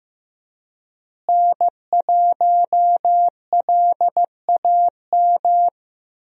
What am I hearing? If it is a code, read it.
Morse code, heard as a keyed tone, about 15 wpm, 700 Hz. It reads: N1LAM
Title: USA Call Signs - 5 Characters - Repeated 3x - 15wpm